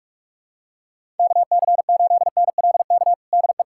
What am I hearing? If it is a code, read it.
Morse code, heard as a keyed tone, about 43 wpm, 700 Hz. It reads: KC8NLK DE
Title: USA Callsign DE Callsign 45wpm